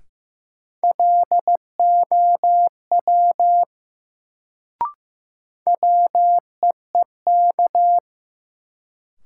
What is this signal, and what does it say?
Morse code, heard as a keyed tone, about 15 wpm, 700 Hz. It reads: LOW WEEK
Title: Sentences from Top 500 Words - New Word Drill - 15wpm